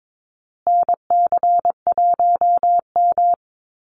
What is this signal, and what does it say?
Morse code, heard as a keyed tone, about 22 wpm, 700 Hz. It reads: NC1M
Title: Call Signs Repeated 2x - 22wpm